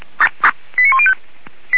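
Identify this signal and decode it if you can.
Five-tone selective call (ZVEI), heard as quick runs of five tones, tones of about 70 ms. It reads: 89105
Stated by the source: ZVEI five-tone call 89136 received by SDR.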